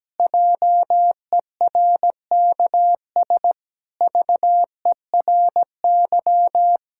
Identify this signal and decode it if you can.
Morse code, heard as a keyed tone, about 17 wpm, 700 Hz. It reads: JERKS VERY
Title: Pangram Challenge v2 - 17wpm